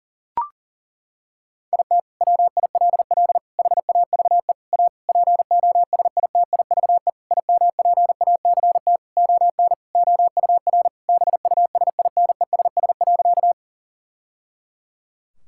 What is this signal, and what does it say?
Morse code, heard as a keyed tone, about 40 wpm, 700 Hz. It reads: IT WILL HAVE A POSITIVE IMPACT ON OUR BUSINESS.
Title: Sentences from Top 1000 Words - Review Part 6 - 40wpm